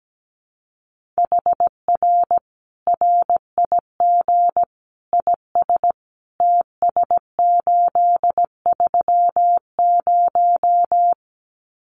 Morse code, heard as a keyed tone, about 17 wpm, 700 Hz. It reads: HR RIG IS TS830
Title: Common QSO Phrases 17wpm